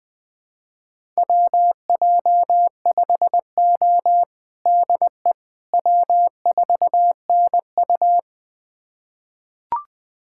Morse code, heard as a keyed tone, about 20 wpm, 700 Hz. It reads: WJ5O DE W4NU
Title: USA Callsign DE Callsign - 4 Characters - 20wpm